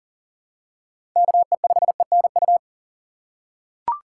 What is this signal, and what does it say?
Morse code, heard as a keyed tone, about 40 wpm, 700 Hz. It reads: KE5ENU
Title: USA Call Signs - 6 Characters - 3x Character Spacing - 40wpm